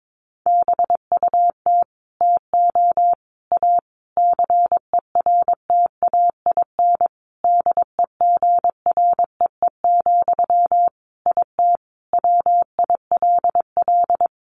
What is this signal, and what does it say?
Morse code, heard as a keyed tone, about 22 wpm, 700 Hz. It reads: BUT TO A CERTAIN DEGREE, IT WILL
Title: Sentences from Top 1500 Words 22wpm